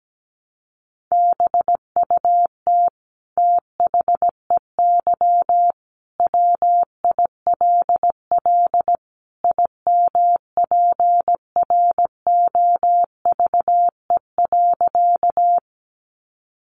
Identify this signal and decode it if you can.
Morse code, heard as a keyed tone, about 17 wpm, 700 Hz. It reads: BUT THEY WILL IMPROVE.
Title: Sentences from Top 800 Words 17wpm